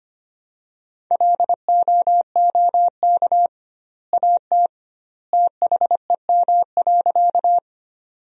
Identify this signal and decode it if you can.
Morse code, heard as a keyed tone, about 25 wpm, 700 Hz. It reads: LOOK AT THEM.
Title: 2x Word-Spacing - Sentences from Top 100 Words - 25wpm